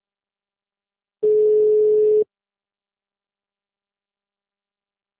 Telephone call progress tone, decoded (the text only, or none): ringback tone